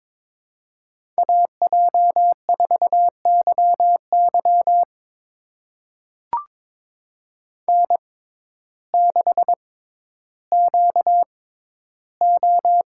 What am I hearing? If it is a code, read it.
Morse code, heard as a keyed tone, about 22 wpm, spaced as for 12 wpm, 700 Hz. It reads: AJ4YY N6QO
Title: USA Call Signs - 5 Characters - 6x Character Spacing - 22wpm